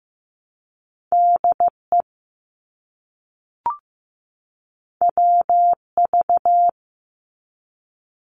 Morse code, heard as a keyed tone, about 15 wpm, 700 Hz. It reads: DE WV
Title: US States 15wpm